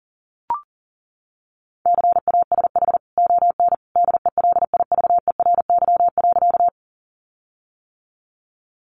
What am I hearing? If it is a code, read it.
Morse code, heard as a keyed tone, about 40 wpm, 700 Hz. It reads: CASH ON DELIVERY.
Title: Sentences from Top 1500 Words 40wpm